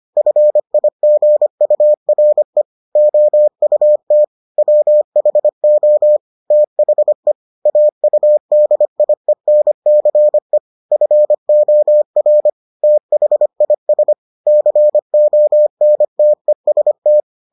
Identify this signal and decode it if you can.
Morse code, heard as a keyed tone, about 25 wpm, 600 Hz. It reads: FIGURE OUT WHO THE AUDIENCE FOR THIS CONTEST